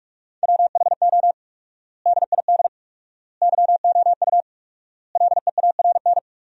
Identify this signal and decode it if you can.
Morse code, heard as a keyed tone, about 46 wpm, 700 Hz. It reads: WHO DID YOU LEARN
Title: 4x Word-Spacing - Sentences from Top 300 Words - 45wpm